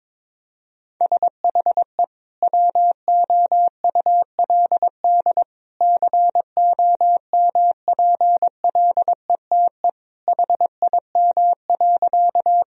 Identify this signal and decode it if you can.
Morse code, heard as a keyed tone, about 22 wpm, 700 Hz. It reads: SHE WOULD COMPLETE HIM.
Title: Sentences from Top 1000 Words - Review Part 2 - 22wpm